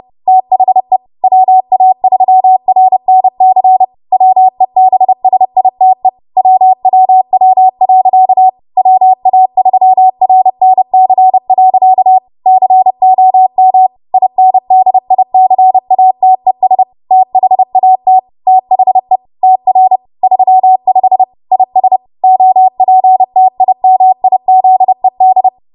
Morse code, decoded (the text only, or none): THE WA3RNC WEBSITE WWW. WA3RNC. COM INDICATES THAT THE TR 35 IS OPTIMIZED